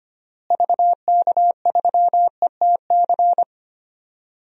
Morse code, heard as a keyed tone, about 25 wpm, 700 Hz. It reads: VK3ETC
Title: DXCC-100 Call Signs - 6 Characters - Repeated 3x - 25wpm